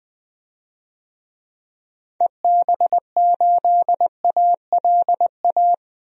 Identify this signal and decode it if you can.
Morse code, heard as a keyed tone, about 20 wpm, 700 Hz. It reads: EB8ALA
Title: DXCC-100 Call Signs - 6 Characters - 20wpm